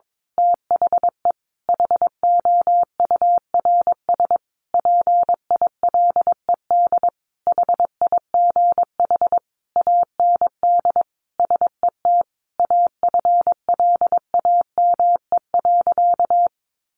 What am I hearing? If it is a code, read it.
Morse code, heard as a keyed tone, about 22 wpm, 700 Hz. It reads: THE HOURS PILED HIGH AND SET AFLAME.
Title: A Sound Of Thunder 22wpm